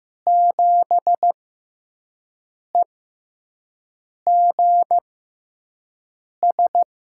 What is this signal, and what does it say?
Morse code, heard as a keyed tone, about 15 wpm, spaced as for 5 wpm, 700 Hz. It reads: 7EGS
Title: USA Call Signs - 6 Characters - 6x Character Spacing - 15wpm